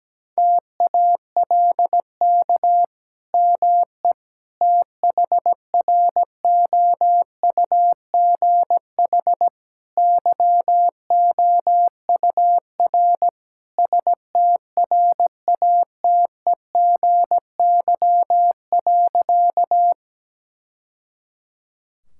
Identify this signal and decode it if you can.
Morse code, heard as a keyed tone, about 17 wpm, 700 Hz. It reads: TALK ME THROUGH YOUR STRATEGY.
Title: Sentences from Top 1000 Words - Review Part 5 - 17wpm